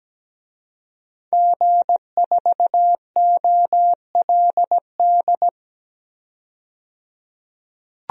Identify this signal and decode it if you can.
Morse code, heard as a keyed tone, about 17 wpm, 700 Hz. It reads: G4OLD
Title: DXCC-100 Call Signs - 5 Characters - 17wpm